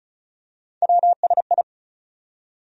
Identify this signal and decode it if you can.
Morse code, heard as a keyed tone, about 36 wpm, 700 Hz. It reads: WSI
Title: USA Call Sign Postfixes - 3 Letters 35wpm